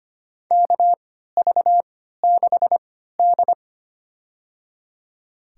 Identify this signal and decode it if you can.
Morse code, heard as a keyed tone, about 25 wpm, spaced as for 14 wpm, 700 Hz. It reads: KV6D
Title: USA Call Signs - 3x Character Spacing - 25wpm